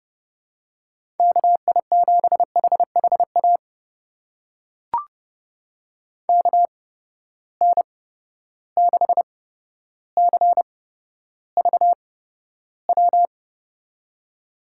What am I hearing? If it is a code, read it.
Morse code, heard as a keyed tone, about 31 wpm, spaced as for 13 wpm, 700 Hz. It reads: KI7HHA KN6CVW
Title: USA Call Signs - 6 Characters - 8x Character Spacing - 30wpm